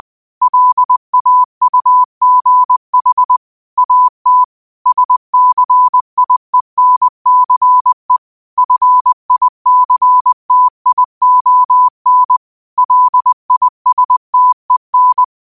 Morse code, heard as a keyed tone, about 20 wpm, 1000 Hz. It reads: LAUGH AT SCIENCE FICTION LISTEN
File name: 20wpm